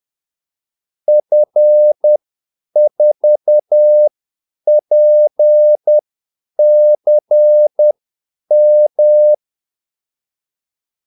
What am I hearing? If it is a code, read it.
Morse code, heard as a keyed tone, about 10 wpm, spaced as for 8 wpm, 600 Hz. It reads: F4PCM
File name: Callsigns-10-8wpm